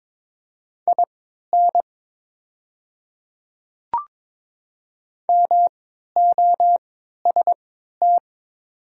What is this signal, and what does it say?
Morse code, heard as a keyed tone, about 21 wpm, spaced as for 12 wpm, 700 Hz. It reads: IN MOST
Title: 3x Character Spacing - Top 100 Words - 22wpm